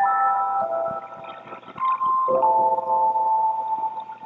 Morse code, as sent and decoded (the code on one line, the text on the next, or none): none